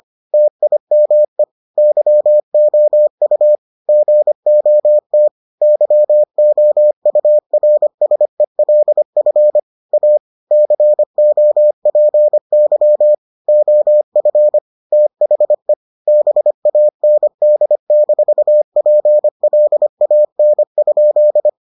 Morse code, heard as a keyed tone, about 25 wpm, 600 Hz. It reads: TIME YOU GOT YOURSELF A COPY OF THE BAND-PLAN?